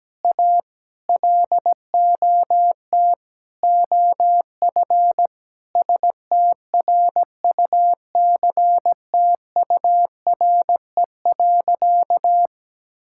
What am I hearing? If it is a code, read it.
Morse code, heard as a keyed tone, about 17 wpm, 700 Hz. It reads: A LOT OF STRUCTURE.
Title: Sentences from Top 1000 Words - Review Part 6 - 17wpm